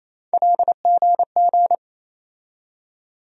Morse code, heard as a keyed tone, about 28 wpm, 700 Hz. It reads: LGG
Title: USA Call Sign Postfixes - 3 Letters 28wpm